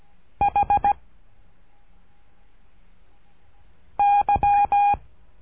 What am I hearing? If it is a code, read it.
Morse code, heard as a keyed tone, about 17 wpm, 810 Hz. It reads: H Y
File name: cursos / CURSO DE TELEGRAFIA / Telegrafia - Trilha 13